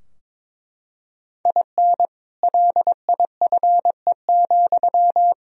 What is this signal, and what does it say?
Morse code, heard as a keyed tone, about 22 wpm, 700 Hz. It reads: IN LIFE,
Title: Sentences from Top 1000 Words - Review Part 4 - 22wpm